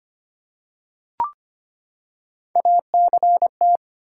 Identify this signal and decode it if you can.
Morse code, heard as a keyed tone, about 25 wpm, 700 Hz. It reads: ACT
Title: Three Letter Words - 25wpm